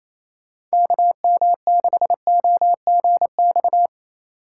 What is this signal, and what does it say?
Morse code, heard as a keyed tone, about 28 wpm, 700 Hz. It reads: KM6OGX
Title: USA Call Signs - 6 Characters - 2x Character Spacing - 28wpm